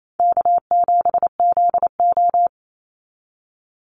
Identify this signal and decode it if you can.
Morse code, heard as a keyed tone, about 28 wpm, 700 Hz. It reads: K7ZO
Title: NA SOTA Chaser Callsigns 28wpm